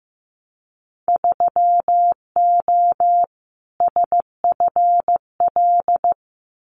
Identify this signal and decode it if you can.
Morse code, heard as a keyed tone, about 15 wpm, 700 Hz. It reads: 3O SFL
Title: Winter Field Day Exchanges 15wpm